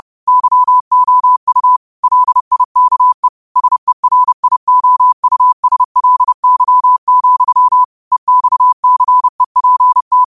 Morse code, heard as a keyed tone, about 30 wpm, 1000 Hz. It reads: YOU LIKE SERIOUSLY, EXCEPT
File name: qotdfiles / 30wpm